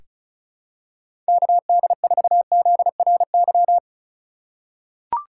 Morse code, heard as a keyed tone, about 34 wpm, 700 Hz. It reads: KD4ZRY
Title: USA Call Signs - 6 Characters - 4x Character Spacing - 35wpm